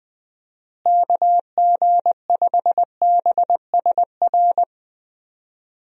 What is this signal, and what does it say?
Morse code, heard as a keyed tone, about 20 wpm, 700 Hz. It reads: KG5BSR
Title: USA Call Signs - 6 Characters - Repeated 3x - 20wpm